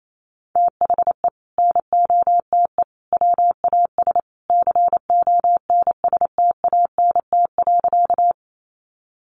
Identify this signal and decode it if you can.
Morse code, heard as a keyed tone, about 28 wpm, 700 Hz. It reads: THE NOTE WAS CONSTANT.
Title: Sentences from Top 1500 Words 28wpm